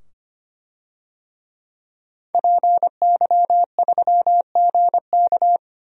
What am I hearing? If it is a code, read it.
Morse code, heard as a keyed tone, about 25 wpm, 690 Hz. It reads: PY3GK
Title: DXCC-100 Call Signs - 5 Characters - 25wpm